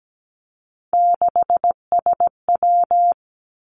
Morse code, heard as a keyed tone, about 17 wpm, 700 Hz. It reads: 6SW
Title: Three Challenging Letter-Number Combinations 17wpm